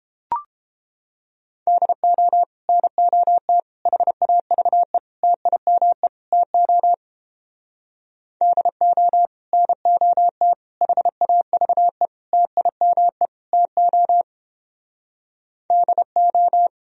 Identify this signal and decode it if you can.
Morse code, heard as a keyed tone, about 31 wpm, 700 Hz. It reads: DO NOT HAVE TIME TO DO NOT HAVE TIME TO DO